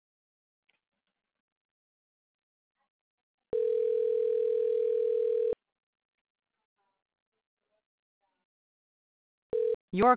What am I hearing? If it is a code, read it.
Ringback tone (North America), 440 + 480 Hz.